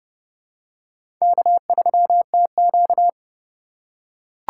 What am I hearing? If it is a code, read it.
Morse code, heard as a keyed tone, about 30 wpm, 700 Hz. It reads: K3TQ
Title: USA Call Signs - 4 Characters - 4x Character Spacing - 30wpm